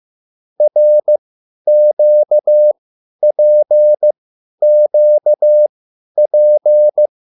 Morse code, heard as a keyed tone, about 15 wpm, spaced as for 10 wpm, 600 Hz. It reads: RQPQP
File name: QSL / CW lesson